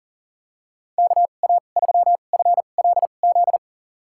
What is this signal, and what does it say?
Morse code, heard as a keyed tone, about 40 wpm, spaced as for 29 wpm, 700 Hz. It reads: KA3FLZ
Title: USA Call Signs - 6 Characters - 2x Character Spacing - 40wpm